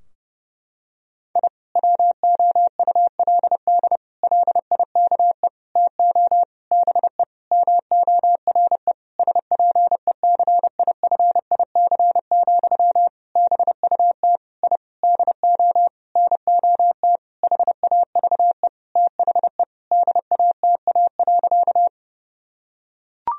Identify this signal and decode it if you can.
Morse code, heard as a keyed tone, about 30 wpm, 700 Hz. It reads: I WOULD LIKE TO BE MORE SPECIFIC, BUT I DO NOT HAVE THE DATA.
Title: Sentences from Top 1000 Words - Review Part 3 - 30wpm